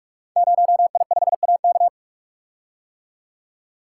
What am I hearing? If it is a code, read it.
Morse code, heard as a keyed tone, about 44 wpm, 690 Hz. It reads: 0I5AK
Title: Code Groups - 5 Letter:Numbers - 45wpm